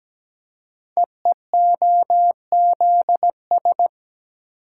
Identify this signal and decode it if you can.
Morse code, heard as a keyed tone, about 17 wpm, 700 Hz. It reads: EEOZS
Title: Code Groups - 5 Letters - 17wpm